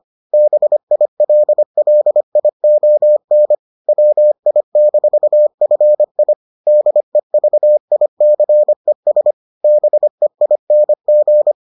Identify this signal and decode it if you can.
Morse code, heard as a keyed tone, about 25 wpm, 600 Hz. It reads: BILLION WI-FI DEVICES BEING